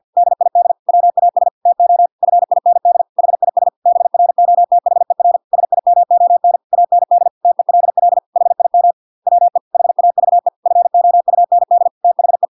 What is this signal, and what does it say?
Morse code, heard as a keyed tone, about 50 wpm, 700 Hz. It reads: DID WAS TO FIND HIS BROTHER SIMON AND TELL HIM WE HAVE FOUND THE